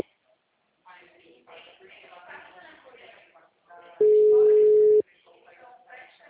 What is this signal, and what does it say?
Ringback tone (Europe, 425 Hz), with speech.
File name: out- -2011- - - .8682